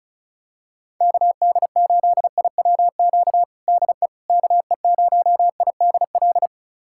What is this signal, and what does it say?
Morse code, heard as a keyed tone, about 36 wpm, 700 Hz. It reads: KD8IWQ DE KE0IDL
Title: USA Callsign DE Callsign - 6 Characters - 35wpm